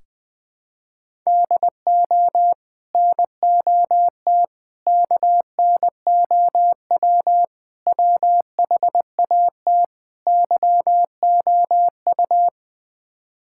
Morse code, heard as a keyed tone, about 21 wpm, 700 Hz. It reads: DO NOT KNOW WHAT YOU
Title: Build A Phrase - 5-Word Phrases - 20wpm